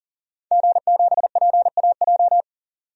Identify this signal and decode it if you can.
Morse code, heard as a keyed tone, about 40 wpm, 700 Hz. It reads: G7PAJ
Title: DXCC-100 Call Signs - 5 Characters - 40wpm